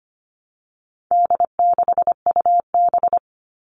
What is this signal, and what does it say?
Morse code, heard as a keyed tone, about 25 wpm, 700 Hz. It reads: D6UB
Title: All Cut Numbers - 4-Digit 25wpm